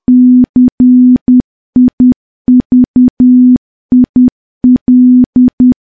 Morse code, heard as a keyed tone, about 10 wpm, 250 Hz. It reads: CIVIL
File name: morse / morse-training-10WPM